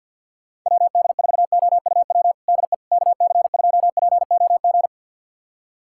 Morse code, heard as a keyed tone, about 51 wpm, 690 Hz. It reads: WD4OUW DE KC2POG
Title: USA Callsign DE Callsign 50wpm